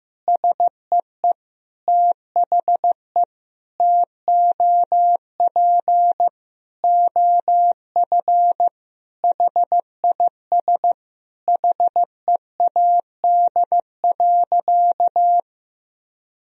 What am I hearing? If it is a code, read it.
Morse code, heard as a keyed tone, about 16 wpm, 700 Hz. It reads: SEE THE TOP OF HIS HEAD.
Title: Sentences from Top 1000 Words - Review Part 2 - 15wpm